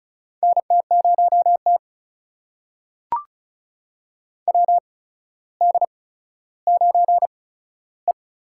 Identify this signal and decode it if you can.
Morse code, heard as a keyed tone, about 36 wpm, spaced as for 15 wpm, 700 Hz. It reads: NT0T WD9E
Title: USA Call Signs - 8x Character Spacing - 35wpm